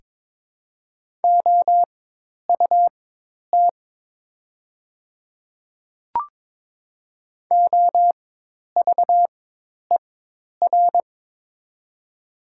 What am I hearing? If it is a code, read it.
Morse code, heard as a keyed tone, about 23 wpm, spaced as for 10 wpm, 690 Hz. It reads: OUT OVER
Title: 4x Character Spacing - Top 200 Words - 22wpm